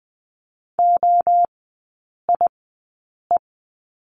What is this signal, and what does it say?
Morse code, heard as a keyed tone, about 20 wpm, spaced as for 8 wpm, 700 Hz. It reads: OIE
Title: 138 Lesson 23e - Letter f - Three Characters 20wpm